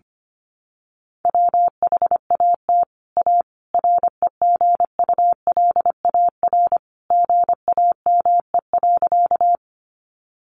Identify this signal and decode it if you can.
Morse code, heard as a keyed tone, about 25 wpm, 700 Hz. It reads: WHAT A REGULAR GAME.
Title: Sentences from Top 900 Words 25wpm